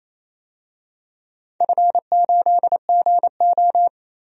Morse code, heard as a keyed tone, about 28 wpm, 700 Hz. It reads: F8GO
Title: DXCC-100 Call Signs - 4 Characters - Repeated 3x - 28wpm